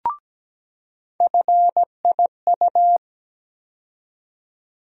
Morse code, heard as a keyed tone, about 17 wpm, 700 Hz. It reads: FIU
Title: USA Call Sign Postfixes - 3 Letters 17wpm